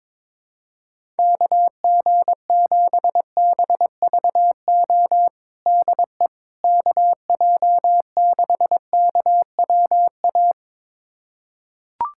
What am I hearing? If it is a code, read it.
Morse code, heard as a keyed tone, about 22 wpm, 700 Hz. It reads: KG7BVO DE KJ6KWA
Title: USA Callsign DE Callsign - 6 Characters - 22wpm